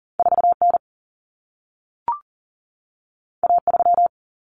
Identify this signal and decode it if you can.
Morse code, heard as a keyed tone, about 40 wpm, 700 Hz. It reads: 4N A3
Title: Common Cut Numbers - 2-Digit 40wpm